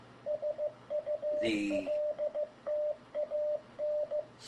Morse code, heard as a keyed tone, about 15 wpm, 600 Hz, with radio noise. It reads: SULTAN
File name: bct / bct1227CQ100Demo